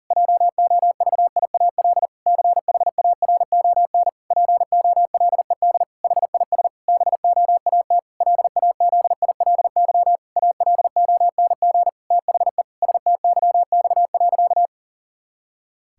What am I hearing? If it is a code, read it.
Morse code, heard as a keyed tone, about 40 wpm, 700 Hz. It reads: JOVIAL CHARON POLED HIS BOAT LAZILY ALONG THE STYX.
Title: Pangram Challenge v2 - 40wpm